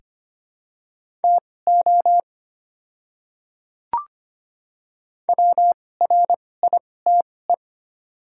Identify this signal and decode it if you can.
Morse code, heard as a keyed tone, about 25 wpm, spaced as for 18 wpm, 700 Hz. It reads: TO WRITE
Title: 2x Character Spacing - Top 200 Words - 25wpm